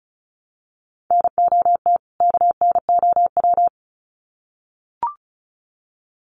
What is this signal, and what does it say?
Morse code, heard as a keyed tone, about 34 wpm, 700 Hz. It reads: NOT KNOW
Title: Sets of 2 Words 35wpm